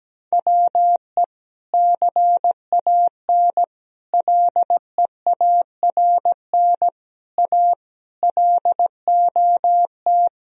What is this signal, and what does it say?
Morse code, heard as a keyed tone, about 17 wpm, 700 Hz. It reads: WE CAN LEARN A LOT
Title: Sentences from Top 1000 Words - Review Part 2 - 17wpm